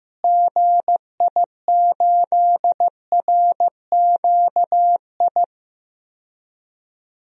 Morse code, heard as a keyed tone, about 15 wpm, 700 Hz. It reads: GI8RQI
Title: DXCC-100 Call Signs - 6 Characters - Repeated 2x - 15wpm